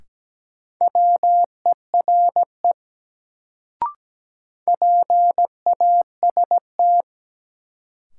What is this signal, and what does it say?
Morse code, heard as a keyed tone, about 17 wpm, 700 Hz. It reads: WERE PAST
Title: Sets of 2 Words - New Word Drill - 17wpm